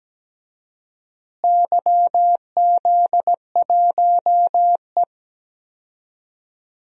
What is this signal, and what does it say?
Morse code, heard as a keyed tone, about 17 wpm, 700 Hz. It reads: YZ1E
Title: DXCC-100 Call Signs - 4 Characters - Repeated 2x - 17wpm